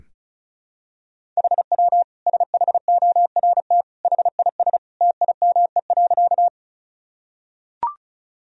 Morse code, heard as a keyed tone, about 36 wpm, 700 Hz. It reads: HW SHORT HIS TIME.
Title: Sentences from Top 400 Words and Abbreviations 35wpm